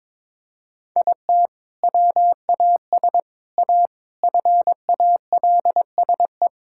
Morse code, heard as a keyed tone, about 22 wpm, 700 Hz. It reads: IT WAS A FALSE
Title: Sentences from Top 1600 Words 22wpm